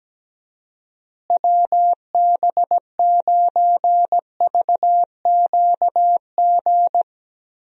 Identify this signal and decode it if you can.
Morse code, heard as a keyed tone, about 17 wpm, 700 Hz. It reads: WB9VQG
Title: USA Call Signs - 6 Characters - Repeated 2x - 17wpm